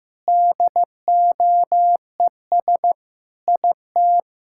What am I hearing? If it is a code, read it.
Morse code, heard as a keyed tone, about 15 wpm, 700 Hz. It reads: DOES IT
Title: Sentences from Top 1600 Words 15wpm